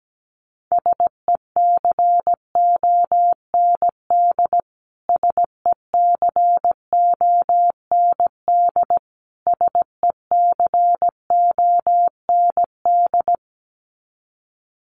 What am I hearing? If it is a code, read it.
Morse code, heard as a keyed tone, about 17 wpm, 700 Hz. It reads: SECOND SECOND SECOND
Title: Top 300 Words Repeated 17wpm